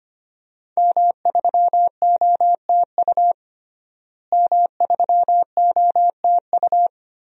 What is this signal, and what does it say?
Morse code, heard as a keyed tone, about 25 wpm, 700 Hz. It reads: M3OTU M3OTU
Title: DXCC-100 Call Signs - 5 Characters - Repeated 3x - 25wpm